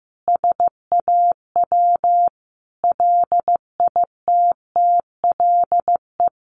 Morse code, heard as a keyed tone, about 15 wpm, 700 Hz. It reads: SAW LITTLE
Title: Sentences from Top 1500 Words 15wpm